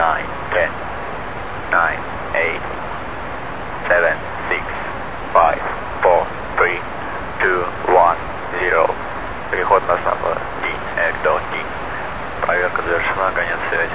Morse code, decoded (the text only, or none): none